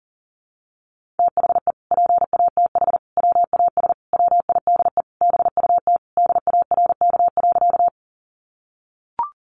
Morse code, heard as a keyed tone, about 40 wpm, 700 Hz. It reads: THE PATH WAS WIDE BUT DARK.
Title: Sentences from Top 1400 Words 40wpm